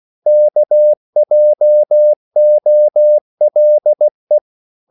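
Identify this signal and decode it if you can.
Morse code, heard as a keyed tone, about 16 wpm, 600 Hz. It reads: KJOLE
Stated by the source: Slow (QRS) Morse code at 16 wpm, from CW med Gnister.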